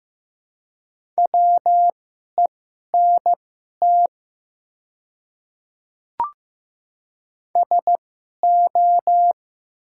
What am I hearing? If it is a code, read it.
Morse code, heard as a keyed tone, about 15 wpm, spaced as for 11 wpm, 700 Hz. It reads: WENT SO
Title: 2x Character Spacing - Top 300 Words - 15wpm